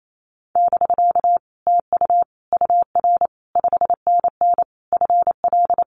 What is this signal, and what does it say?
Morse code, heard as a keyed tone, about 28 wpm, 700 Hz. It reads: <BK> TU UR 5NN FL
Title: Common QSO Phrases Encore 28wpm